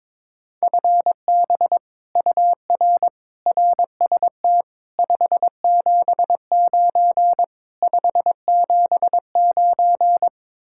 Morse code, heard as a keyed tone, about 22 wpm, 700 Hz. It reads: FB UR RST 579 579
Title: Speed-Racing Common QSO Phrases Encore - 22wpm